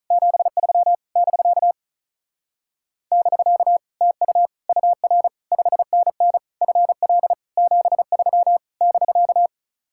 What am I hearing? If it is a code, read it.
Morse code, heard as a keyed tone, about 47 wpm, 700 Hz. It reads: 73 <BK> <BK> TU UR 5NN FL 73 <BK>